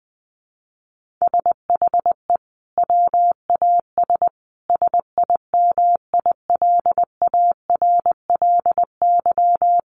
Morse code, heard as a keyed tone, about 20 wpm, 700 Hz. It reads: SHE WAS SIMILARLY
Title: Sentences from Top 1400 Words 20wpm